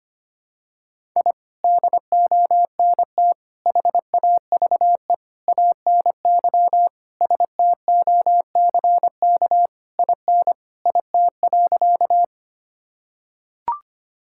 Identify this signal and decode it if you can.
Morse code, heard as a keyed tone, about 25 wpm, 700 Hz. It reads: I DONT HAVE ANY STOCK IN IT.
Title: Sentences from Top 1000 Words - Review Part 5 - 25wpm